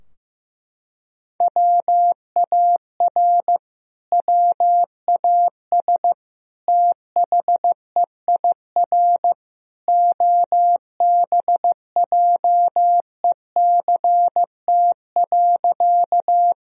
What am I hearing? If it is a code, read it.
Morse code, heard as a keyed tone, about 15 wpm, 700 Hz. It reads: WAR WAS THEIR OBJECT.
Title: Sentences from Top 1000 Words - Review Part 3 - 15wpm